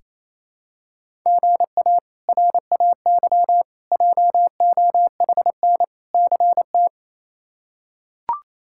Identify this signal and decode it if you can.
Morse code, heard as a keyed tone, about 28 wpm, 700 Hz. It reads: GA RAY JOHN CT
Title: K1USN SST QSO Practice - 28wpm